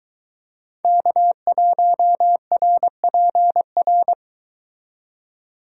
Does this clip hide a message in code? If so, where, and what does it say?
Yes: Morse code from 0.8 to 4.1 s, heard as a keyed tone, about 22 wpm, 700 Hz. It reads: K1RPR